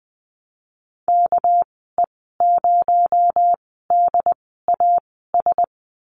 Morse code, heard as a keyed tone, about 19 wpm, spaced as for 14 wpm, 700 Hz. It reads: KE0DAS
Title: 219 Lesson 35g - Number 0 - Call Signs 20wpm